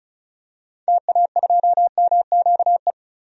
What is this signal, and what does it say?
Morse code, heard as a keyed tone, about 36 wpm, 700 Hz. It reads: TA2MQE
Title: DXCC-100 Call Signs - 6 Characters - 35wpm